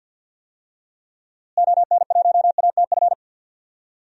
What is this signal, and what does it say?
Morse code, heard as a keyed tone, about 51 wpm, 700 Hz. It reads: ON1ATF
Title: DXCC-100 Call Signs - 6 Characters - Repeated 3x - 50wpm